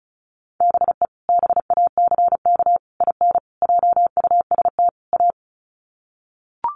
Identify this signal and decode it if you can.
Morse code, heard as a keyed tone, about 34 wpm, 700 Hz. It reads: BE BACK IN JUST A
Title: Sets of 5 Words Encore 35wpm